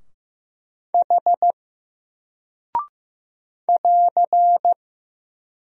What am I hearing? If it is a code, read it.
Morse code, heard as a keyed tone, about 15 wpm, 700 Hz. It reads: H <AR>
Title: Single Character (Letter:Number:Punctuation:Prosign) 15wpm